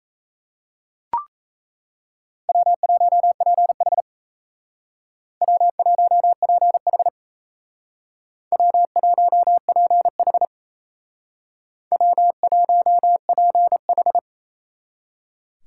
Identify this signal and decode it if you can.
Morse code, heard as a keyed tone, about 35 wpm, 700 Hz. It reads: W1PH W1PH W1PH W1PH